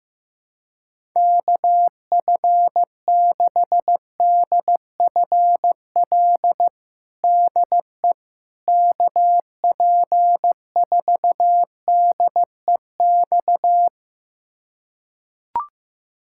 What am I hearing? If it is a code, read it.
Morse code, heard as a keyed tone, about 15 wpm, 700 Hz. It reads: KF6DFL DE KP4DEX
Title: USA Callsign DE Callsign - 6 Characters - 15wpm